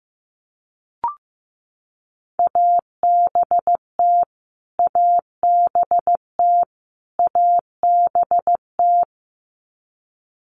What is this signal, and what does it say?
Morse code, heard as a keyed tone, about 15 wpm, 700 Hz. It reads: ABT ABT ABT
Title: Abbreviations Repeated 3x - 15wpm